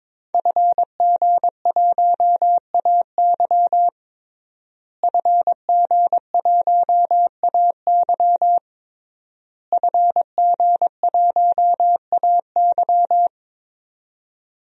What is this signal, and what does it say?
Morse code, heard as a keyed tone, about 22 wpm, 700 Hz. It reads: FG1AY FG1AY FG1AY
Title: DXCC-100 Call Signs - 5 Characters - Repeated 3x - 22wpm